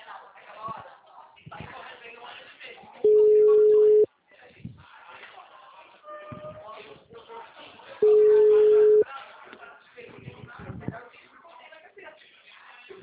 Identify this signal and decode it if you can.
Ringback tone (Europe, 425 Hz), with speech.
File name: out- -2053- - - .66506